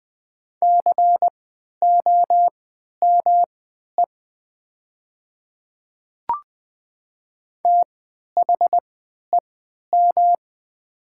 Morse code, heard as a keyed tone, about 20 wpm, spaced as for 11 wpm, 700 Hz. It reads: COME THEM
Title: 3x Character Spacing - Top 100 Words - 20wpm